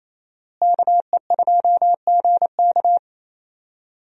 Morse code, heard as a keyed tone, about 28 wpm, 700 Hz. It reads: KE2GK
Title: USA Call Signs - 5 Characters - Repeated 3x - 28wpm